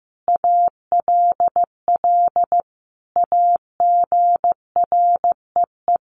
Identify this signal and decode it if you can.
Morse code, heard as a keyed tone, about 14 wpm, 700 Hz. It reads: ALL AGREE
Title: Sentences from Top 1100 Words 15wpm